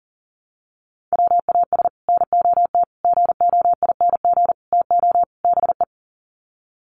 Morse code, heard as a keyed tone, about 40 wpm, 700 Hz. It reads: WAS NOT GOING TO BE
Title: Sets of 5 Words Encore 40wpm